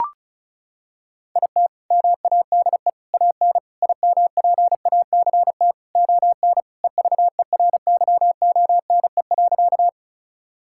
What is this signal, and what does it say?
Morse code, heard as a keyed tone, about 36 wpm, 700 Hz. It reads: IT MADE AN IMPACT ON EVERYONE.
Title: Sentences from Top 1000 Words - Review Part 6 - 35wpm